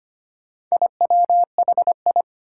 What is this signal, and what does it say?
Morse code, heard as a keyed tone, about 25 wpm, 700 Hz. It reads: IWHI
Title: Four Challenging Letter-Number Combinations 25wpm